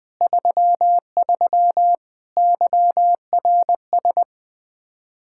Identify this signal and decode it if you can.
Morse code, heard as a keyed tone, about 20 wpm, 700 Hz. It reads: 33 YRS
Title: Practice QSO - Exchange 3 - More Info - 20wpm